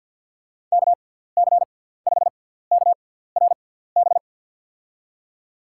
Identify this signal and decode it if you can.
Morse code, heard as a keyed tone, about 52 wpm, spaced as for 17 wpm, 700 Hz. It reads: KC5KRB
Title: USA Call Signs - 6 Characters - 6x Character Spacing - 50wpm